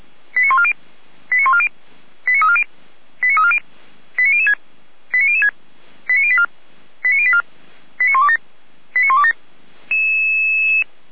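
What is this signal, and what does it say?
Five-tone selective call (ZVEI), heard as quick runs of five tones, tones of about 70 ms. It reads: 89140 89140 89340 89340 89006 89006 89084 89084 89127 89127
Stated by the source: ZVEI five-tone call received by SDR, from feuerwehr-orlamuende.org fire brigade archive.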